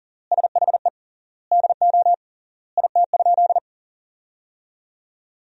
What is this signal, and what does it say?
Morse code, heard as a keyed tone, about 40 wpm, 700 Hz. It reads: SHE DO IT?
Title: 3x Word-Spacing - Sentences from Top 100 Words - 40wpm